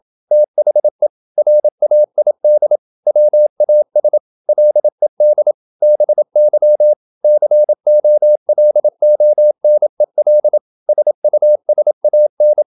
Morse code, heard as a keyed tone, about 27 wpm, 600 Hz. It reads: THE RAID WAS LED BY COLONEL SUSAN
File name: Never 0027wpm